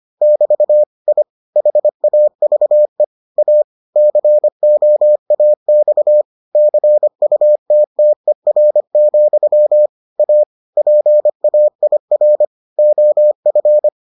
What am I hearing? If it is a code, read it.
Morse code, heard as a keyed tone, about 25 wpm, 600 Hz. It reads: <BT> I HAVE A COAX CUTTER, A PAIR OF